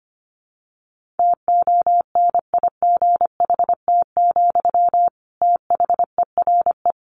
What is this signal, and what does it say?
Morse code, heard as a keyed tone, about 25 wpm, 700 Hz. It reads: TONIGHT, THERE
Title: Sentences from Top 1400 Words 25wpm